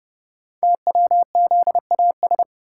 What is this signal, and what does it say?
Morse code, heard as a keyed tone, about 28 wpm, 700 Hz. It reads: TWZAS
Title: Code Groups - 5 Letters - 30wpm